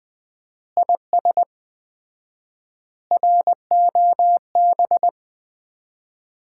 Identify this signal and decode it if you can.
Morse code, heard as a keyed tone, about 20 wpm, 700 Hz. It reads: IS ROB
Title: Common QSO Phrases - 4x Word-Spacing - 20wpm